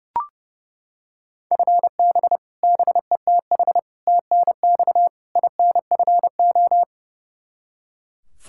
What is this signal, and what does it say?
Morse code, heard as a keyed tone, about 32 wpm, 700 Hz. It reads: FB BETH TNX INFO
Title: Practice QSO - Exchange 3 - More Info - 30wpm